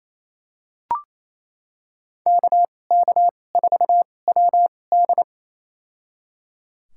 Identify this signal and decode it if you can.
Morse code, heard as a keyed tone, about 28 wpm, spaced as for 20 wpm, 700 Hz. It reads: KK4WD
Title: USA Call Signs - 5 Characters - 2x Character Spacing - 28wpm